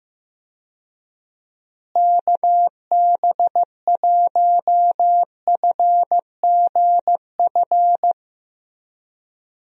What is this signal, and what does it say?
Morse code, heard as a keyed tone, about 15 wpm, 700 Hz. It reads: KB1FGF
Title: USA Call Signs - 6 Characters - Repeated 2x - 15wpm